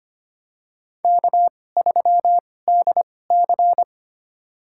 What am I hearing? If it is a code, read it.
Morse code, heard as a keyed tone, about 25 wpm, spaced as for 18 wpm, 700 Hz. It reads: K3DC
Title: USA Call Signs - 4 Characters - 2x Character Spacing - 25wpm